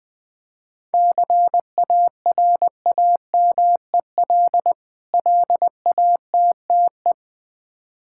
Morse code, heard as a keyed tone, about 21 wpm, 700 Hz. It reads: CARAMEL LATTE
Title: Difficult Words with Hints - 20wpm